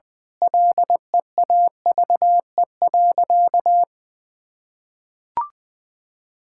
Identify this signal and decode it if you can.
Morse code, heard as a keyed tone, about 20 wpm, 700 Hz. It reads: LEAVE.
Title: Sentences with 5-Word Phrases - 20wpm